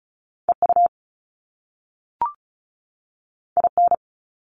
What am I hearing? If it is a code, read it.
Morse code, heard as a keyed tone, about 36 wpm, 700 Hz. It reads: EU IN
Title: Top Two Letter Combinations in Words 35wpm